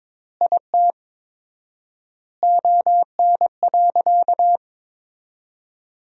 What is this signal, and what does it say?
Morse code, heard as a keyed tone, about 22 wpm, 700 Hz. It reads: IT ON.
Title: 4x Word-Spacing - Sentences from Top 200 Words - 22wpm